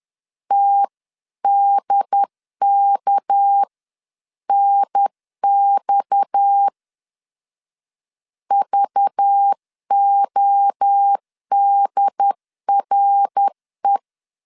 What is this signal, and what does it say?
Morse code, heard as a keyed tone, about 10 wpm, 800 Hz. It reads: TDKNX VODRE
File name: CW / DET / 1 6 1500C S C 10wpm 4252